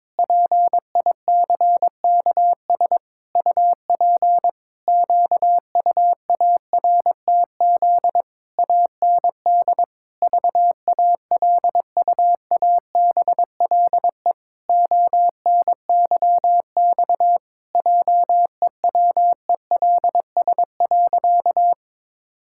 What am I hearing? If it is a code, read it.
Morse code, heard as a keyed tone, about 22 wpm, 700 Hz. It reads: PICKS UP QUARTZ AND VALUABLE ONYX JEWELS.
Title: Pangram Challenge v2 - 22wpm